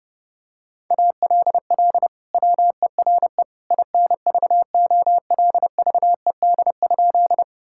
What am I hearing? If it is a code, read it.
Morse code, heard as a keyed tone, about 32 wpm, 700 Hz. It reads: ALL WERE INVOLVED?
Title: Sentences from Top 1000 Words - Review Part 5 - 30wpm